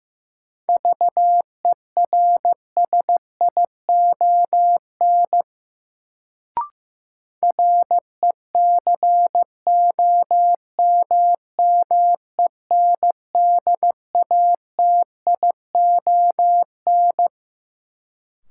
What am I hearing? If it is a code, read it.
Morse code, heard as a keyed tone, about 15 wpm, 700 Hz. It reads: VERSION RECOMMENDATION
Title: Sentences from Top 1400 Words - New Word Drill Alt - 15wpm